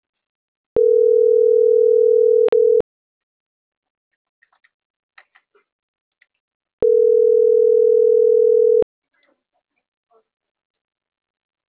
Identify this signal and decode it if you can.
Ringback tone (North America), 440 + 480 Hz.